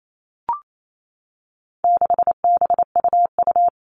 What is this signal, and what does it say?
Morse code, heard as a keyed tone, about 28 wpm, 700 Hz. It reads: 6BUU
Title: All Cut Numbers - 4-Digit 28wpm